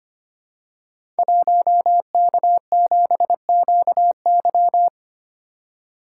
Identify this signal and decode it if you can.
Morse code, heard as a keyed tone, about 25 wpm, 700 Hz. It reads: 1K7QY
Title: Code Groups - 5 Letter:Numbers - 25wpm